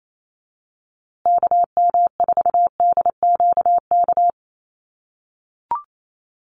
Morse code, heard as a keyed tone, about 28 wpm, 700 Hz. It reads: KM4DQK
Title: USA Call Signs Repeated 2x 28wpm